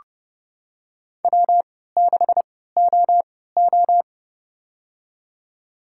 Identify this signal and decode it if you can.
Morse code, heard as a keyed tone, about 29 wpm, spaced as for 17 wpm, 700 Hz. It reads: W6OO
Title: USA Call Signs - 4 Characters - 3x Character Spacing - 30wpm